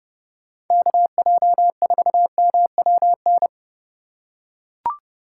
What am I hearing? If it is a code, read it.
Morse code, heard as a keyed tone, about 29 wpm, 700 Hz. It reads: KJ4MWN